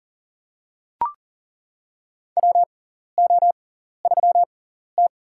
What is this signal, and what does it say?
Morse code, heard as a keyed tone, about 40 wpm, spaced as for 14 wpm, 700 Hz. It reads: WO3T
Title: USA Call Signs - 4 Characters - 6x Character Spacing - 40wpm